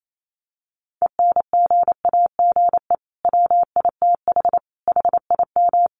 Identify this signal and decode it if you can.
Morse code, heard as a keyed tone, about 28 wpm, 700 Hz. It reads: ENGAGE WITH HIM
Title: Sentences from Top 1500 Words 28wpm